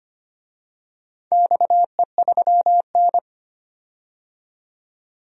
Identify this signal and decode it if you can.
Morse code, heard as a keyed tone, about 25 wpm, 700 Hz. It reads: XE3N
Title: DXCC-100 Call Signs - 4 Characters - 25wpm